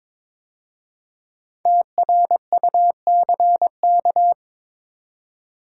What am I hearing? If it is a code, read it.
Morse code, heard as a keyed tone, about 22 wpm, 700 Hz. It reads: TRUCK
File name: 5 Letter Words - First 3 Letters Given - 22wpm